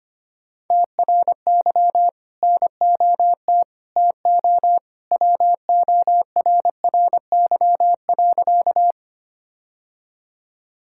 Morse code, heard as a keyed tone, about 25 wpm, 700 Hz. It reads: TRY NOT TO WORRY.
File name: practice / 25wpm / Sentences with 2-Word Phrases - 25wpm